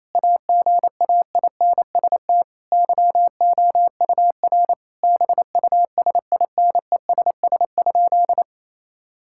Morse code, heard as a keyed tone, about 27 wpm, 700 Hz. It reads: AGAINST YOUR BUSINESS?
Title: Sentences from Top 1000 Words - Review Part 5 - 28wpm